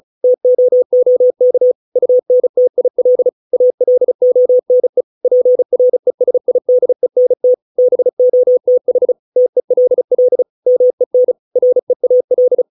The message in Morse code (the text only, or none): TOOK UNTIL ALONE PRESIDENT BOTH TELL MEN REAL